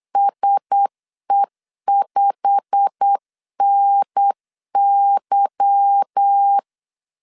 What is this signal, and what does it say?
Morse code, heard as a keyed tone, about 8 wpm, 800 Hz. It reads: SE5NY
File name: LEZIONI / 8 YJ45 / 8 YJ45 8wpm 3214